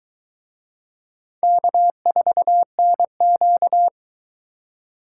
Morse code, heard as a keyed tone, about 22 wpm, 700 Hz. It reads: K4NQ